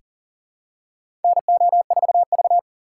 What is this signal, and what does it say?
Morse code, heard as a keyed tone, about 39 wpm, 690 Hz. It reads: NO4V
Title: USA Call Signs - 4 Characters - 2x Character Spacing - 40wpm